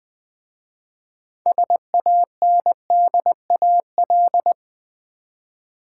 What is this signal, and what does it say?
Morse code, heard as a keyed tone, about 20 wpm, 700 Hz. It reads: SANDAL
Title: Easy Words with Hints - 20wpm